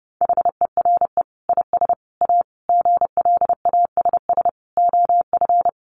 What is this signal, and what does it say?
Morse code, heard as a keyed tone, about 30 wpm, 700 Hz. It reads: HERE IS A GLASS OF
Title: Sentences from Top 1300 Words 30wpm